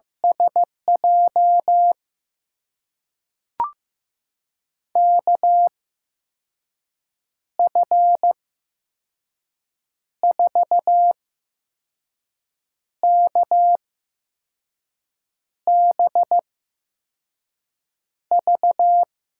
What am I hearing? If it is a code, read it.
Morse code, heard as a keyed tone, about 15 wpm, spaced as for 5 wpm, 700 Hz. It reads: SJ KF4KBV
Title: USA Call Signs - 6 Characters - 8x Character Spacing - 15wpm